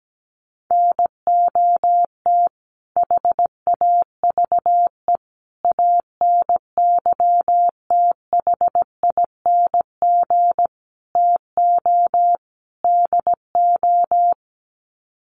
Morse code, heard as a keyed tone, about 17 wpm, 700 Hz. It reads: NOT HAVE ANYTHING TO DO
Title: Sets of 5 Words Encore 17wpm